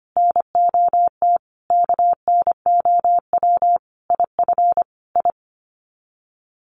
Morse code, heard as a keyed tone, about 25 wpm, 700 Hz. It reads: NOT KNOW IF I
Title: Sets of 5 Words 25wpm